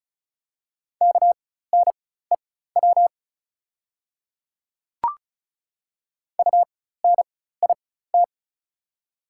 Morse code, heard as a keyed tone, about 36 wpm, spaced as for 16 wpm, 690 Hz. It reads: KNEW UNIT
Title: 4x Character Spacing - Top 500 Words - 35wpm